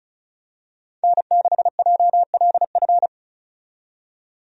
Morse code, heard as a keyed tone, about 36 wpm, 700 Hz. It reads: N6JLF
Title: USA Call Signs - 5 Characters - 35wpm